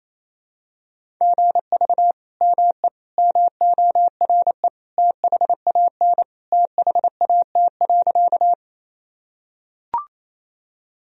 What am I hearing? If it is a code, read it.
Morse code, heard as a keyed tone, about 28 wpm, 700 Hz. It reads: GV ME MORE THAN THAT.
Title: Sentences from Top 200 Words and Abbreviations 28wpm